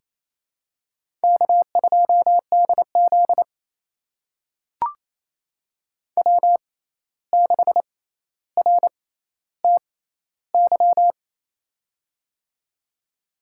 Morse code, heard as a keyed tone, about 29 wpm, spaced as for 14 wpm, 700 Hz. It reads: K2DZ W6RTY
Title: USA Call Signs - 6x Character Spacing - 28wpm